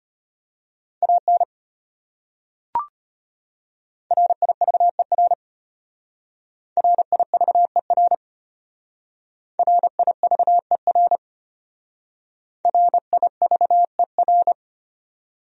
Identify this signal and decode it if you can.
Morse code, heard as a keyed tone, about 33 wpm, 700 Hz. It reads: AN RIVER RIVER RIVER RIVER